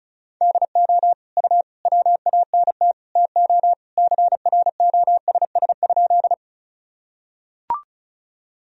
Morse code, heard as a keyed tone, about 36 wpm, 700 Hz. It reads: DO U WANT TO CROSS?
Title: Sentences from Top 300 Words and Abbreviations 35wpm